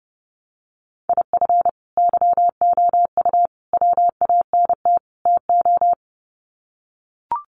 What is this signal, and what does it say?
Morse code, heard as a keyed tone, about 30 wpm, 700 Hz. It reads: IF YOU WANT TO
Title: Sets of 4 Words 30wpm